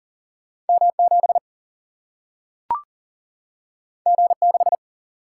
Morse code, heard as a keyed tone, about 40 wpm, 700 Hz. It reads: M 7 G 6
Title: Single-Letter Single-Number 40wpm